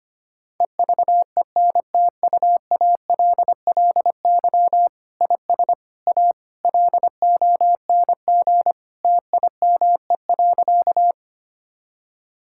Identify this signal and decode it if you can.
Morse code, heard as a keyed tone, about 25 wpm, 700 Hz. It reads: EVENTUALLY IS A LONG TIME.
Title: Sentences from Top 1000 Words - Review Part 6 - 25wpm